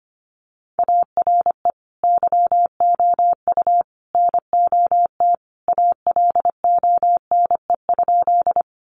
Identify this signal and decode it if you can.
Morse code, heard as a keyed tone, about 25 wpm, 700 Hz. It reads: ARE YOU NOT ALONE?
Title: Sentences from Top 1000 Words 25wpm